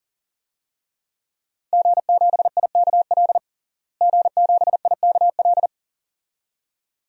Morse code, heard as a keyed tone, about 40 wpm, 700 Hz. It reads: G7IKL G7IKL
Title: DXCC-100 Call Signs - 5 Characters - Repeated 2x - 40wpm